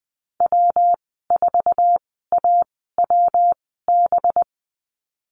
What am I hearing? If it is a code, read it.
Morse code, heard as a keyed tone, about 20 wpm, spaced as for 14 wpm, 700 Hz. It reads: W4AWB
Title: 231 Lesson 37g - Letter x - Call Signs 20wpm